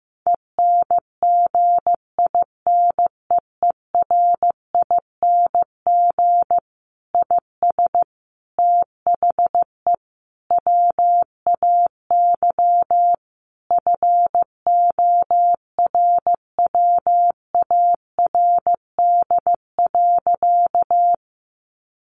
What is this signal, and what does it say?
Morse code, heard as a keyed tone, about 15 wpm, 700 Hz. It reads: ENGINEERING IS THE WAY FORWARD.
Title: Sentences from Top 1500 Words 15wpm